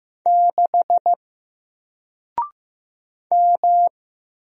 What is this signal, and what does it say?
Morse code, heard as a keyed tone, about 15 wpm, 700 Hz. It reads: 6 M
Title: Single Letter-Number Alt 15wpm